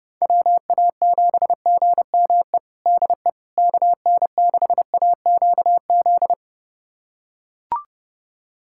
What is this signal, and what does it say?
Morse code, heard as a keyed tone, about 32 wpm, 700 Hz. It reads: WA7GME DE KN6AQZ
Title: USA Callsign DE Callsign 30wpm